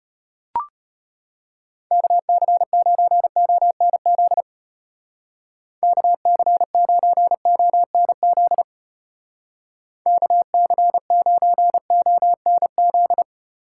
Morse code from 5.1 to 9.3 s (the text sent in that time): KC9ONZ